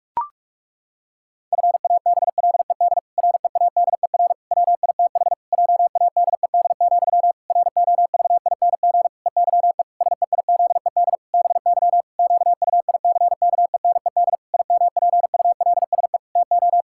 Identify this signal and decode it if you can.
Morse code, heard as a keyed tone, about 44 wpm, 700 Hz. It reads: FABLED READER WITH JADED, ROVING EYE SEIZED BY QUICKENED IMPULSE TO
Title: Pangram Challenge v2 - 45wpm